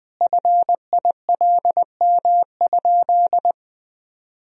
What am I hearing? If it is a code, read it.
Morse code, heard as a keyed tone, about 21 wpm, 700 Hz. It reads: FILM?
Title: Sentences from Top 1000 Words - Review Part 3 - 20wpm